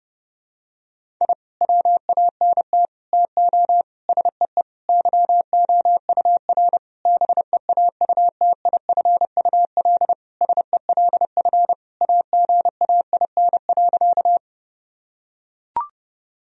Morse code, heard as a keyed tone, about 30 wpm, 700 Hz. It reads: I WANT TO SEE YOUR BEAUTIFUL SELF AGAIN.
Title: Sentences from Top 1000 Words - Review Part 1 - 30wpm